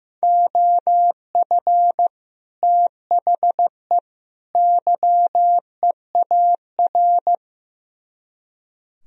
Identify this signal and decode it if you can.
Morse code, heard as a keyed tone, about 15 wpm, 700 Hz. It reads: OF THE YEAR
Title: Speed-Racing - Sets of 3 Words 15wpm